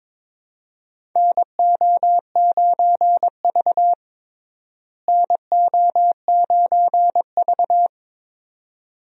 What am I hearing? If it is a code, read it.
Morse code, heard as a keyed tone, about 22 wpm, 700 Hz. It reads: NO9V NO9V
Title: USA Call Signs - 4 Characters - Repeated 2x - 22wpm